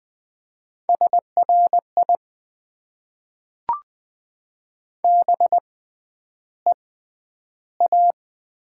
Morse code, heard as a keyed tone, about 20 wpm, spaced as for 10 wpm, 700 Hz. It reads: SRI BEA
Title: Top 90 QSO Elements - 6x Character Spacing - 20wpm